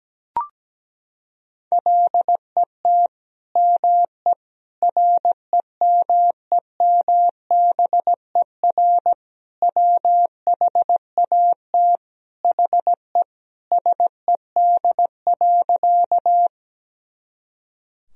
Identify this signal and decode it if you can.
Morse code, heard as a keyed tone, about 17 wpm, 700 Hz. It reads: LET ME REMEMBER WHAT HE SED.
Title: Sentences from Top 400 Words and Abbreviations 17wpm